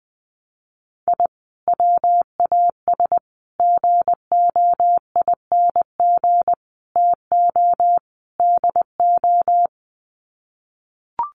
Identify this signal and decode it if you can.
Morse code, heard as a keyed tone, about 20 wpm, 700 Hz. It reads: I WAS GOING TO DO
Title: Sets of 5 Words Encore2 20wpm